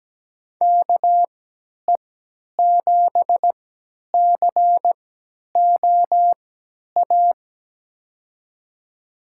Morse code, heard as a keyed tone, about 17 wpm, spaced as for 10 wpm, 700 Hz. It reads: KE7COA
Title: USA Call Signs - 6 Characters - 3x Character Spacing - 17wpm